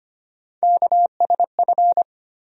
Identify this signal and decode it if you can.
Morse code, heard as a keyed tone, about 25 wpm, 700 Hz. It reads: KSF
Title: USA Call Sign Postfixes - 3 Letters 25wpm